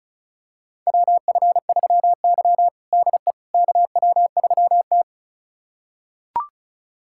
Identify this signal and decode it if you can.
Morse code, heard as a keyed tone, about 36 wpm, 700 Hz. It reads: WF3Y DE KW3T
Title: USA Callsign DE Callsign - 4 Characters - 35wpm